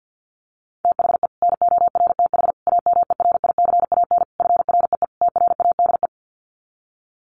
Morse code, heard as a keyed tone, about 51 wpm, 700 Hz. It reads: THE NORTH AMERICAN FREE TRADE
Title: Sets of 5 Words Encore 50wpm